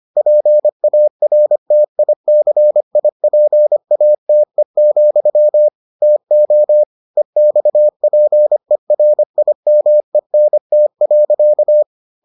Morse code, heard as a keyed tone, about 25 wpm, 600 Hz. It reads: PARTICIPATE, TO EXPERIMENT.